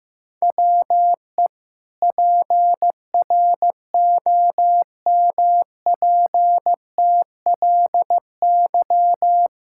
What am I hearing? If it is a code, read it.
Morse code, heard as a keyed tone, about 15 wpm, 700 Hz. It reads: WE PROMPTLY
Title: Pangram Challenge v2 - 15wpm